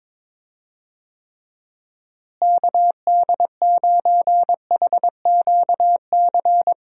Morse code, heard as a keyed tone, about 22 wpm, 700 Hz. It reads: KD9HQC
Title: USA Call Signs - 6 Characters - Repeated 3x - 22wpm